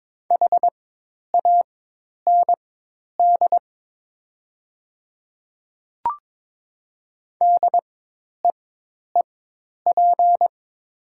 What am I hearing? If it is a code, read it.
Morse code, heard as a keyed tone, about 22 wpm, spaced as for 10 wpm, 700 Hz. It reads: HAND DEEP
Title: 4x Character Spacing - Top 500 Words - 22wpm